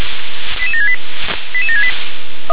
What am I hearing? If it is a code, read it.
Five-tone selective call (ZVEI), heard as quick runs of five tones, tones of about 70 ms. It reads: 88768 88768
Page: SDR/ZVEI